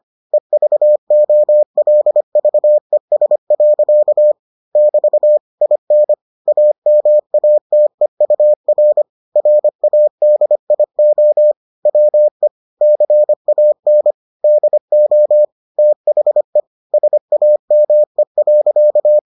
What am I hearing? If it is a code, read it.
Morse code, heard as a keyed tone, about 25 wpm, 600 Hz. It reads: EVOLVES. <BT> IN AMATEUR RADIO WE CAN DO THE SAME.